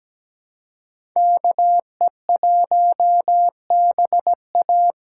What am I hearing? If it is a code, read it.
Morse code, heard as a keyed tone, about 20 wpm, 700 Hz. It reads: KE1BA
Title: Speed Racing - USA 2x2 Call Signs 17wpm